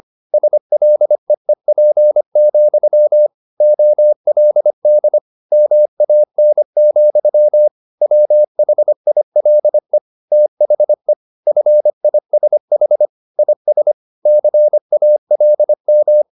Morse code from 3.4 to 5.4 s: OLD

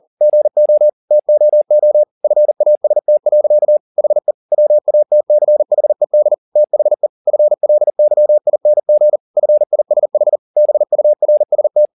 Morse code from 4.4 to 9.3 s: WATCHED THE FLYING

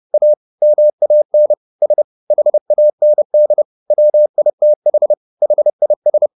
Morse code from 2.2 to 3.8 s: HAND

